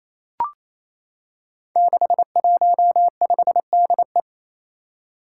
Morse code, heard as a keyed tone, about 28 wpm, 700 Hz. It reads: 615DE
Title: Five Challenging Letter-Number Combinations 28wpm